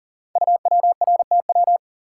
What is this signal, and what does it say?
Morse code, heard as a keyed tone, about 39 wpm, 700 Hz. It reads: UWRTW
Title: Code Groups - 5 Letters - 40wpm